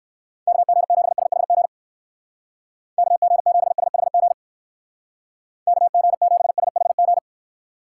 Morse code, heard as a keyed tone, about 59 wpm, 700 Hz. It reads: KG7SHG KG7SHG KG7SHG